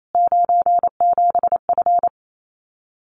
Morse code, heard as a keyed tone, about 28 wpm, 700 Hz. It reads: 97F
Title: Common QSO Phrases Encore 28wpm